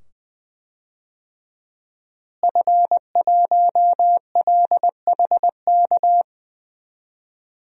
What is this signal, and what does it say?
Morse code, heard as a keyed tone, about 20 wpm, 700 Hz. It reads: F1LHK
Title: DXCC-100 Call Signs - 5 Characters - Repeated 2x - 20wpm